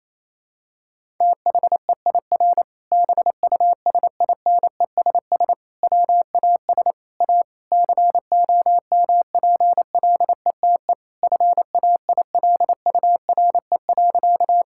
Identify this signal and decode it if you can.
Morse code, heard as a keyed tone, about 28 wpm, 700 Hz. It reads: THEIR BUSINESS WAS A COMPLETE FAILURE.
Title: Sentences from Top 1000 Words - Review Part 6 - 28wpm